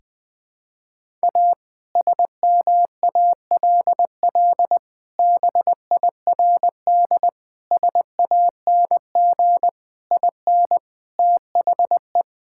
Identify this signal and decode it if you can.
Morse code, heard as a keyed tone, about 20 wpm, 700 Hz. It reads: A SMALL BIRD SANG IN THE
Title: Sentences with 2-Word Phrases - Spoken Beginning - 20wpm